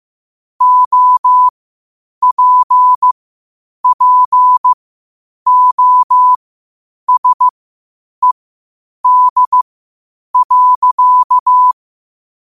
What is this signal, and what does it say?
Morse code, heard as a keyed tone, about 15 wpm, spaced as for 8 wpm, 1000 Hz. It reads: OPPOSED.
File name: qotdfiles / 8wpm